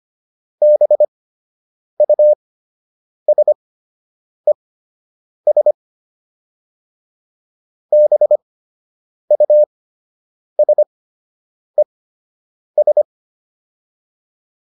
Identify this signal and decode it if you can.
Morse code, heard as a keyed tone, about 25 wpm, spaced as for 8 wpm, 600 Hz. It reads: BUSES BUSES